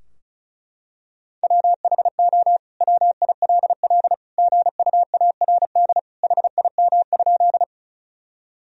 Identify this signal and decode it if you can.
Morse code, heard as a keyed tone, about 34 wpm, 700 Hz. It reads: WHO WILL GUARD HIM?
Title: Sentences from Top 1000 Words - Review Part 4 - 35wpm